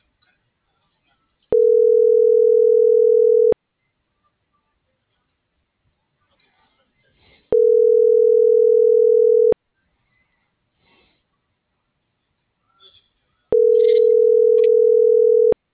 Ringback tone (North America), 440 + 480 Hz.